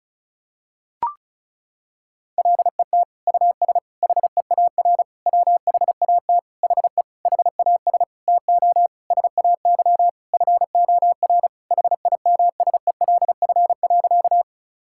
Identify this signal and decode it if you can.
Morse code, heard as a keyed tone, about 36 wpm, 700 Hz. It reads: LET US HEAR WHAT HE HAS TO SAY FOR HIMSELF.
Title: Sentences from Top 1600 Words 35wpm